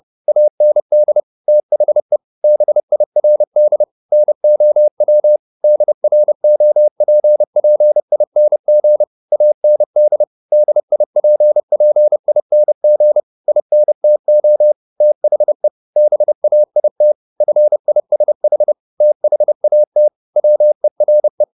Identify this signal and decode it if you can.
Morse code, heard as a keyed tone, about 30 wpm, 600 Hz. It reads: AND THE BIRD NOW DROPPING AND DIPPING INTO THE BAIT FISH THAT WERE
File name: the old man and the sea / the old man and the sea 1 30wpm